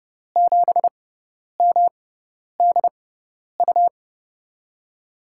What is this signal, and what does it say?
Morse code, heard as a keyed tone, about 32 wpm, spaced as for 10 wpm, 700 Hz. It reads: 7MDU
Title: USA Call Signs - 6 Characters - 6x Character Spacing - 30wpm